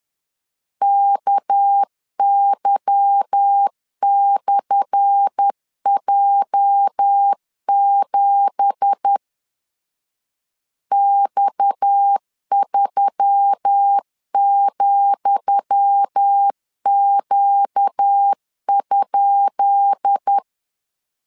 Morse code, heard as a keyed tone, about 10 wpm, 800 Hz. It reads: KY/J7 X3,Q?
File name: LEZIONI / 10 SIGNS / 10D COMA 10wpm 3013